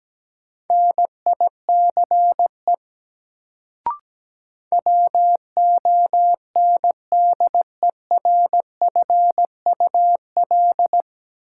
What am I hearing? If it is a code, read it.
Morse code, heard as a keyed tone, about 17 wpm, 700 Hz. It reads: NICE WONDERFUL
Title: Sentences from Top 1100 Words - New Word Drill - 17wpm